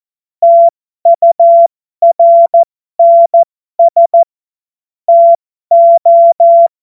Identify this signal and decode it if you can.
Morse code, heard as a keyed tone, about 14 wpm, spaced as for 12 wpm, 680 Hz. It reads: TURNS TO
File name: Morsekurs / Witze